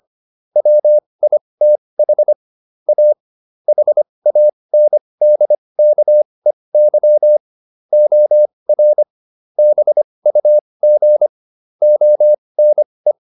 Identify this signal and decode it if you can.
Morse code, heard as a keyed tone, about 25 wpm, spaced as for 20 wpm, 600 Hz. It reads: WITH A HANDKEY OR BUG ONE